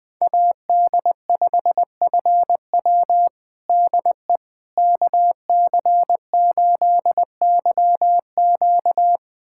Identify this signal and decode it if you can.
Morse code, heard as a keyed tone, about 20 wpm, 700 Hz. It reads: AD5FW DE KC8YQ
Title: USA Callsign DE Callsign - 5 Characters - 20wpm